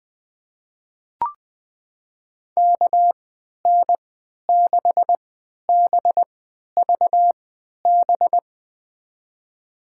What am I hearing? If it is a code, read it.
Morse code, heard as a keyed tone, about 20 wpm, spaced as for 11 wpm, 700 Hz. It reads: KN6BVB
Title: USA Call Signs - 6 Characters - 3x Character Spacing - 20wpm